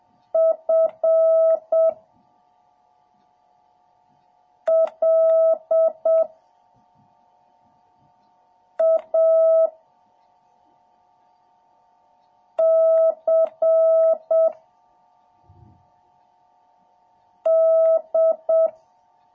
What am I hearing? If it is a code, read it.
Morse code, heard as a keyed tone, about 7 wpm, 650 Hz. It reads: F L A C D